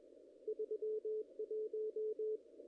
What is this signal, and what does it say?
Morse code, heard as a keyed tone, about 23 wpm, 440 Hz. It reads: 31
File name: contests / ss-cw-k5tr-op-n5rz -0000